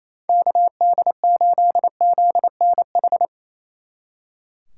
Morse code, heard as a keyed tone, about 35 wpm, 700 Hz. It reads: KD8ZNH
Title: Speed Racing - USA 2x3 Call Signs 28wpm